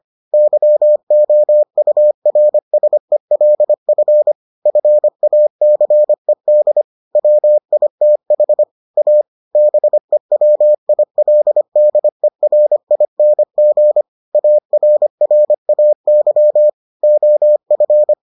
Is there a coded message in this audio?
Yes: Morse code from 0.3 to 18.1 s, heard as a keyed tone, about 25 wpm, 600 Hz. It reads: YOURSELF FACED WITH A BEWILDERING ARRAY OF